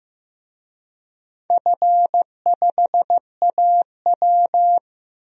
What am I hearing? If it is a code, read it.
Morse code, heard as a keyed tone, about 16 wpm, 700 Hz. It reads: F5AW
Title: DXCC-100 Call Signs - 4 Characters - Repeated 3x - 15wpm